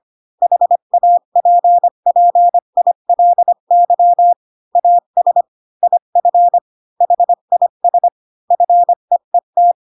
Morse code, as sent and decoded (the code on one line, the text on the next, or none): .... .- .--. .--. .. .-.. -.-- / .- ... / .. ..-. / .... .. ... / ..-. . -
HAPPILY AS IF HIS FEET